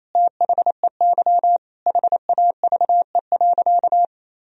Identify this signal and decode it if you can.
Morse code, heard as a keyed tone, about 28 wpm, 700 Hz. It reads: THEY HAVE.
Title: Sentences from Top 1000 Words - Review Part 2 - 28wpm